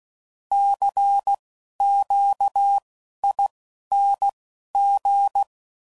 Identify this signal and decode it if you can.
Morse code, heard as a keyed tone, about 16 wpm, spaced as for 11 wpm, 800 Hz. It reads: CQING